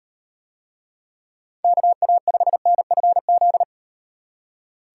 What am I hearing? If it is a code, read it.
Morse code, heard as a keyed tone, about 38 wpm, 700 Hz. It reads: KA5NFZ